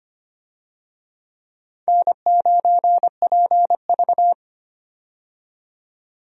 Morse code, heard as a keyed tone, about 25 wpm, 700 Hz. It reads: N9PV
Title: USA Call Signs - 4 Characters - 25wpm